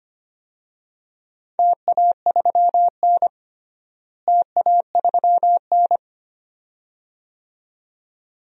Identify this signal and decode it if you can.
Morse code, heard as a keyed tone, about 25 wpm, 700 Hz. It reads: TA3N TA3N
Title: DXCC-100 Call Signs - 4 Characters - Repeated 2x - 25wpm